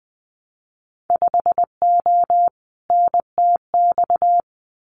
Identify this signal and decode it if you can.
Morse code, heard as a keyed tone, about 20 wpm, 700 Hz. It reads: 5O NTX
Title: Winter Field Day Exchanges Encore 20wpm